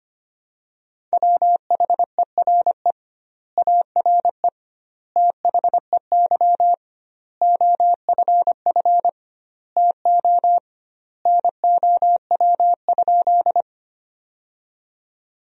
Morse code, heard as a keyed tone, about 25 wpm, 700 Hz. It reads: WHERE ARE THEY OFF TO NOW?
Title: 2x Word-Spacing - Sentences from Top 200 Words - 25wpm